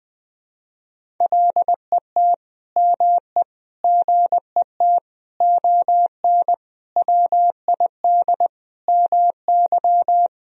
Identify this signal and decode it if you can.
Morse code, heard as a keyed tone, about 20 wpm, 700 Hz. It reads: LET ME GET ON WID MY
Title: Sentences from Top 300 Words and Abbreviations 20wpm